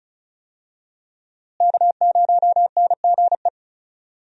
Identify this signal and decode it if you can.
Morse code, heard as a keyed tone, about 36 wpm, 700 Hz. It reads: K0NGE
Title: USA Call Signs - 5 Characters - Repeated 2x - 35wpm